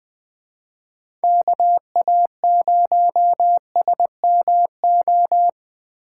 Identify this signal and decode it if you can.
Morse code, heard as a keyed tone, about 21 wpm, 700 Hz. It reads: KA0SMO
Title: USA Call Signs - 6 Characters - 2x Character Spacing - 20wpm